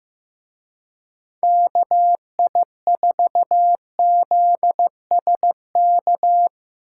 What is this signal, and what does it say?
Morse code, heard as a keyed tone, about 17 wpm, 700 Hz. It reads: KI4ZSK
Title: Speed Racing - USA 2x3 Call Signs 15wpm